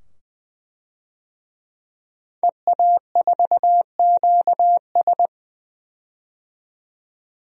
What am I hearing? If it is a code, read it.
Morse code, heard as a keyed tone, about 20 wpm, 700 Hz. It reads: EA4QS
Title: DXCC-100 Call Signs - 5 Characters - 20wpm